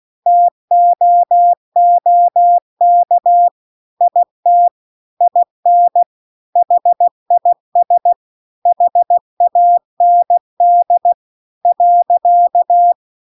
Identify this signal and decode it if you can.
Morse code, heard as a keyed tone, about 16 wpm, 700 Hz. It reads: TOOK IT IN HIS HAND .